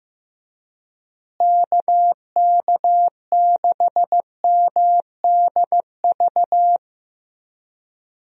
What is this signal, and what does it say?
Morse code, heard as a keyed tone, about 15 wpm, 700 Hz. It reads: KK6MDV
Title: USA Call Signs - 6 Characters - Repeated 3x - 15wpm